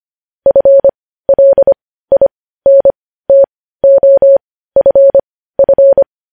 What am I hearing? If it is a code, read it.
Morse code, heard as a keyed tone, about 25 wpm, spaced as for 15 wpm, 550 Hz. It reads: FLINTOFF
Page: news headlines in morse code at 15 wpm